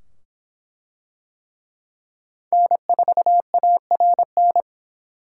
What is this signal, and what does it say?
Morse code, heard as a keyed tone, about 23 wpm, 700 Hz. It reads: N4ARN